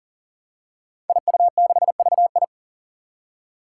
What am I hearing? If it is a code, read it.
Morse code, heard as a keyed tone, about 40 wpm, 700 Hz. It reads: IU6VI
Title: Five Challenging Letter-Number Combinations 40wpm